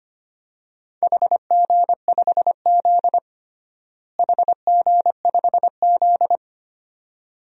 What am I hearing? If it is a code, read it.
Morse code, heard as a keyed tone, about 25 wpm, 700 Hz. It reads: HG5Z HG5Z
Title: DXCC-100 Call Signs - 4 Characters - Repeated 2x - 25wpm